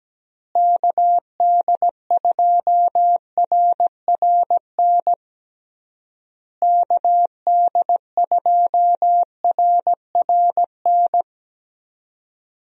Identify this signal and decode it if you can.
Morse code, heard as a keyed tone, about 17 wpm, 700 Hz. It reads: KD2RRN KD2RRN
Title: USA Call Signs - 6 Characters - Repeated 3x - 17wpm